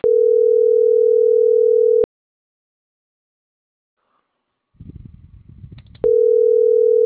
Ringback tone (North America), 440 + 480 Hz.